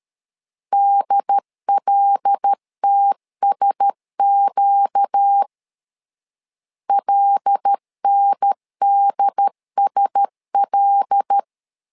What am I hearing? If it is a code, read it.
Morse code, heard as a keyed tone, about 13 wpm, 800 Hz. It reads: DLTSQ LNDSL
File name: LEZIONI / 5 LBVC / 5 L 12wpm 1913